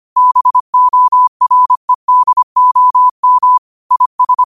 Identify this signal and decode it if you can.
Morse code, heard as a keyed tone, about 25 wpm, 1000 Hz. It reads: BOREDOM IS
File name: qotdfiles / 25wpm